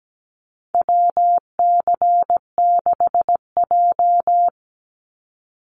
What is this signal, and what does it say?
Morse code, heard as a keyed tone, about 17 wpm, 700 Hz. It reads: WC6J
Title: NA SOTA Activator Callsigns 17wpm